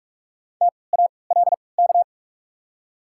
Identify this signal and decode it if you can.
Morse code, heard as a keyed tone, about 44 wpm, 690 Hz. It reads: T A L K
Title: 3x Character Spacing - Top 500 Words - 45wpm